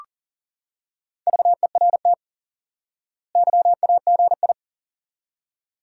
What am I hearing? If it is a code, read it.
Morse code, heard as a keyed tone, about 37 wpm, 700 Hz. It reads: VERT YAGI
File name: IQE Course 3 - Lesson 89 - YAGI - Two Elements - 40wpm